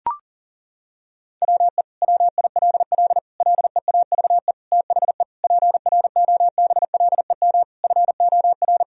Morse code, heard as a keyed tone, about 40 wpm, 700 Hz. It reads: WE WILL LEAVE THE PROBLEM FOR
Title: Sentences from Top 1000 Words - Review Part 2 - 40wpm